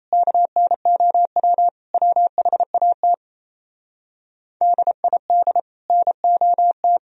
Morse code, heard as a keyed tone, about 36 wpm, 700 Hz. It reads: KNOW WHAT DID NOT